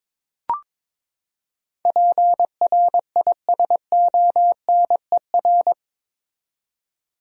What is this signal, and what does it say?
Morse code, heard as a keyed tone, about 22 wpm, 700 Hz. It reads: PRISONER
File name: practice / 22wpm / Eight Letter Words - 22wpm